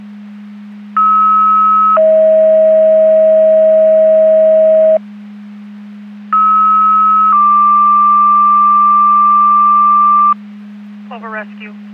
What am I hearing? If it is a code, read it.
Two-tone page, heard as a one-second tone, then a three-second tone, about 1290 then 640 Hz, about 1290 then 1180 Hz, with speech. It is heard from 1.0 to 10.3 s.